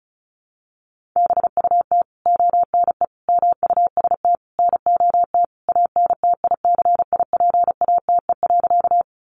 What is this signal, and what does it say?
Morse code, heard as a keyed tone, about 34 wpm, 700 Hz. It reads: BUT ONE MUST NOT ANTICIPATE.
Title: Sentences from Top 1500 Words 35wpm